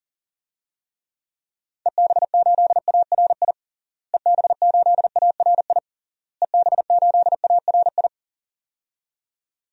Morse code, heard as a keyed tone, about 41 wpm, 700 Hz. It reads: EB8ARI EB8ARI EB8ARI
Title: DXCC-100 Call Signs - 6 Characters - Repeated 3x - 40wpm